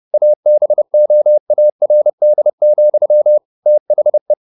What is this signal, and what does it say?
Morse code, heard as a keyed tone, about 30 wpm, 600 Hz. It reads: ABOARD, THE
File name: the old man and the sea / the old man and the sea 1 30wpm